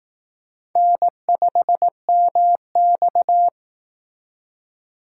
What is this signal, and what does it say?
Morse code, heard as a keyed tone, about 18 wpm, 700 Hz. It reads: N5MX
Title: Speed Racing - USA 1x2 Call Signs 15wpm